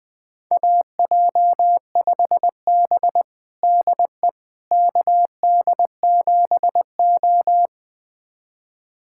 Morse code, heard as a keyed tone, about 20 wpm, 700 Hz. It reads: AJ5B DE KD7O
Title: USA Callsign DE Callsign - 4 Characters - 20wpm